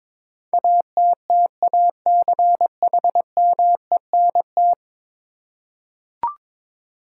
Morse code, heard as a keyed tone, about 22 wpm, 700 Hz. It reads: ATTACHMENT
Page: morse-code-ninja-practice-22wpm